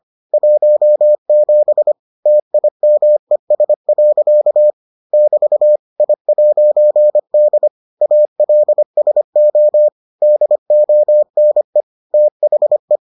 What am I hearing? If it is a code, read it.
Morse code, heard as a keyed tone, about 25 wpm, 600 Hz. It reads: 17 TIMES. <BT> I'D ALSO DONE THE